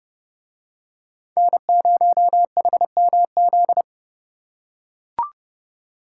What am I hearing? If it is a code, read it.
Morse code, heard as a keyed tone, about 29 wpm, 700 Hz. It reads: N0HMZ